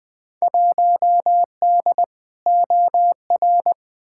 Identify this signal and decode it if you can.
Morse code, heard as a keyed tone, about 20 wpm, 700 Hz. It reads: 1D OR
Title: Field Day Exchanges - 20wpm